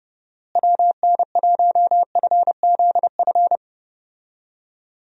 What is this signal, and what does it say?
Morse code, heard as a keyed tone, about 29 wpm, 700 Hz. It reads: WN1FZF
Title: USA Call Signs - 6 Characters - Repeated 2x - 30wpm